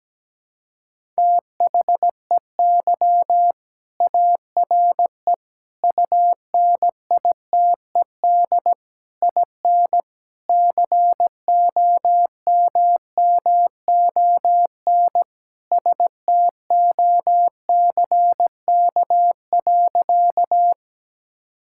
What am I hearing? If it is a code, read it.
Morse code, heard as a keyed tone, about 17 wpm, 700 Hz. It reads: THEY ARE UNITED IN COMMON STOCK.
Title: Sentences from Top 1000 Words - Review Part 5 - 17wpm